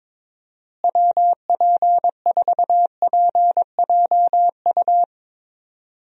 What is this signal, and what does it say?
Morse code, heard as a keyed tone, about 22 wpm, 700 Hz. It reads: WP4PJU
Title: USA Call Signs - 6 Characters - Repeated 2x - 22wpm